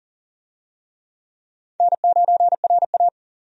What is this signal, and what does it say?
Morse code, heard as a keyed tone, about 40 wpm, 700 Hz. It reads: N9RA
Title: USA Call Signs - 4 Characters - 40wpm